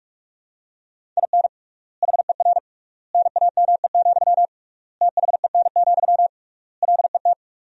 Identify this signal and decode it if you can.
Morse code, heard as a keyed tone, about 42 wpm, 700 Hz. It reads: IN HER NAME, THEN, LET
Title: 3x Word-Spacing - Sentences from Top 500 Words - 45wpm